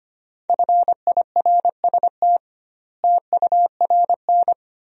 Morse code, heard as a keyed tone, about 25 wpm, 700 Hz. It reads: FIRST TURN
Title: 2x Word-Spacing - Sentences from Top 300 Words - 25wpm